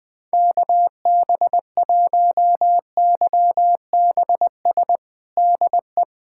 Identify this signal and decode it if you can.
Morse code, heard as a keyed tone, about 20 wpm, 700 Hz. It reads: KB1YBS DE
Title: USA Callsign DE Callsign - 6 Characters - 20wpm